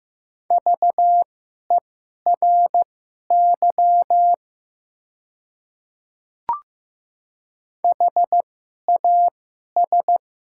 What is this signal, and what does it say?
Morse code, heard as a keyed tone, about 15 wpm, spaced as for 11 wpm, 700 Hz. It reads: VERY HAS
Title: 2x Character Spacing - Top 200 Words - 15wpm